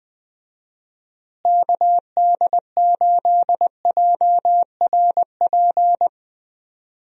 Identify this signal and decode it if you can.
Morse code, heard as a keyed tone, about 20 wpm, 700 Hz. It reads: KD8JRP
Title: USA Call Signs - 6 Characters - Repeated 3x - 20wpm